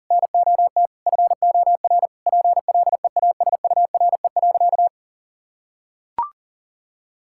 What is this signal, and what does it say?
Morse code, heard as a keyed tone, about 40 wpm, 700 Hz. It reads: NOT FOR PLEASURE.
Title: Sentences from Top 1600 Words 40wpm